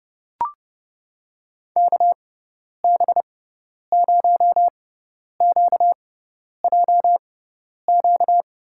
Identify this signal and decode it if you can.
Morse code, heard as a keyed tone, about 32 wpm, spaced as for 10 wpm, 700 Hz. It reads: KB0QJQ
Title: USA Call Signs - 6 Characters - 6x Character Spacing - 30wpm